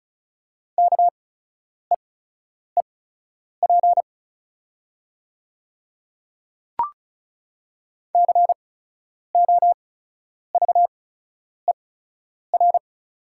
Morse code, heard as a keyed tone, about 35 wpm, spaced as for 10 wpm, 690 Hz. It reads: KEEP COVER
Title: 8x Character Spacing - Top 300 Words - 35wpm